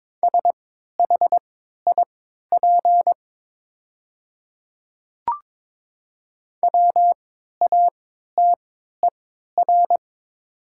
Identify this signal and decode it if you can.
Morse code, heard as a keyed tone, about 22 wpm, spaced as for 12 wpm, 700 Hz. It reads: SHIP WATER
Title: 3x Character Spacing - Top 500 Words - 22wpm